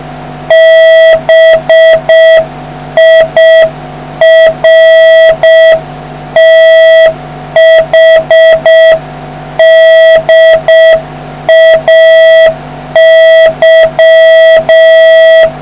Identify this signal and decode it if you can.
Morse code, heard as a keyed tone, about 6 wpm, 650 Hz, with noise. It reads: BIRTHDAY